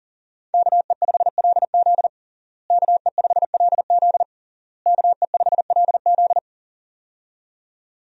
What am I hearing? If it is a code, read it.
Morse code, heard as a keyed tone, about 40 wpm, 700 Hz. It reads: KE5LZ KE5LZ KE5LZ
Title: USA Call Signs - 5 Characters - Repeated 3x - 40wpm